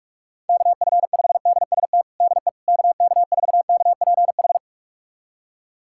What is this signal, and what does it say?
Morse code, heard as a keyed tone, about 46 wpm, 700 Hz. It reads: KF5DST DE KK4KPH
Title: USA Callsign DE Callsign 45wpm